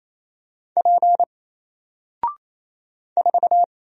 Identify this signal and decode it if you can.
Morse code, heard as a keyed tone, about 27 wpm, 700 Hz. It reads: P 4
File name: Single-Letter-Number-28wpm